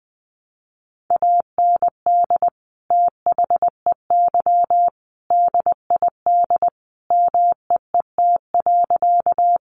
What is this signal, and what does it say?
Morse code, heard as a keyed tone, about 20 wpm, 700 Hz. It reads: AND THEY DID MEET.
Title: Sentences from Top 1100 Words 20wpm